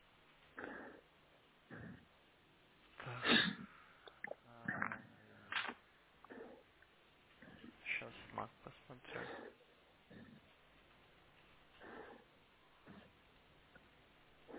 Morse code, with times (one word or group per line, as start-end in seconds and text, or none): none